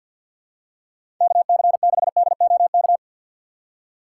Morse code, heard as a keyed tone, about 47 wpm, 700 Hz. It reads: KC6DOK
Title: USA Call Signs - 6 Characters - 6x Character Spacing - 50wpm